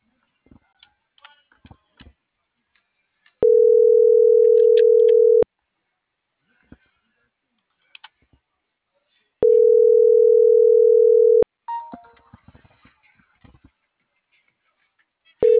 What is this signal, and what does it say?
Ringback tone (North America), 440 + 480 Hz.